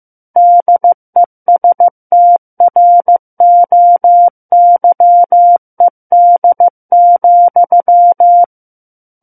Morse code, heard as a keyed tone, about 15 wpm, 700 Hz. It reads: DESTROYED,
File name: Book One / 06-Chapter 6 - 15 WPM